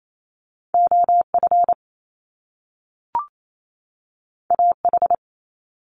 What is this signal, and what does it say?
Morse code, heard as a keyed tone, about 28 wpm, 700 Hz. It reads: OF AH
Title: Top Two Letter Combinations in Words 28wpm